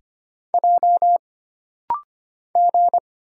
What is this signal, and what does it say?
Morse code, heard as a keyed tone, about 25 wpm, 700 Hz. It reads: J G
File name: Single Character (Letter:Number:Punctuation:Prosign) - Rapid-Fire - 25wpm